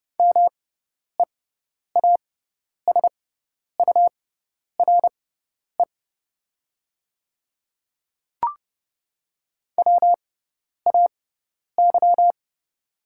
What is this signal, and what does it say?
Morse code, heard as a keyed tone, about 30 wpm, spaced as for 10 wpm, 690 Hz. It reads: MEASURE WAY
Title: 6x Character Spacing - Top 500 Words - 30wpm